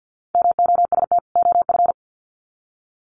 Morse code, heard as a keyed tone, about 50 wpm, 700 Hz. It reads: MOST OF
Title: Sets of 2 Words 50wpm